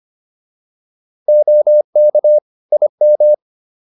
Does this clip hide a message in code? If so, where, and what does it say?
Yes: Morse code from 1.3 to 3.3 s, heard as a keyed tone, about 25 wpm, 600 Hz. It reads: OK IM